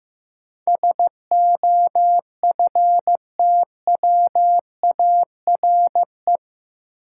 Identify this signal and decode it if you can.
Morse code, heard as a keyed tone, about 15 wpm, 700 Hz. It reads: SOFTWARE
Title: Sentences from Top 1000 Words - Review Part 3 - New Word Drill - 15wpm